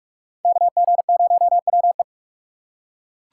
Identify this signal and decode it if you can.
Morse code, heard as a keyed tone, about 41 wpm, 700 Hz. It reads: KG0WE